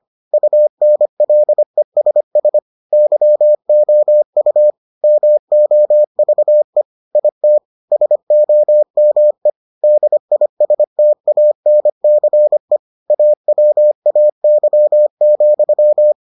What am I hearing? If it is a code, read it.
Morse code, heard as a keyed tone, about 25 wpm, 600 Hz. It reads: UNLESS YOU MOVE IT SOME DISTANCE AWAY,